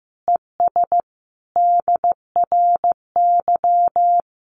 Morse code, heard as a keyed tone, about 15 wpm, 700 Hz. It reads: ES DRY
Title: Common QSO Phrases 15wpm